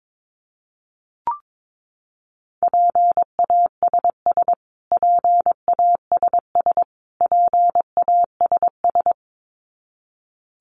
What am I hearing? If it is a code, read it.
Morse code, heard as a keyed tone, about 22 wpm, 700 Hz. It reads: PASS PASS PASS
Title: Top 500 Words Repeated 22wpm